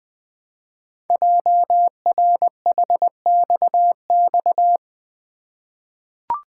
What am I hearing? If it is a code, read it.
Morse code, heard as a keyed tone, about 20 wpm, 700 Hz. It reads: JRHXX
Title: Code Groups - 5 Letters - 20wpm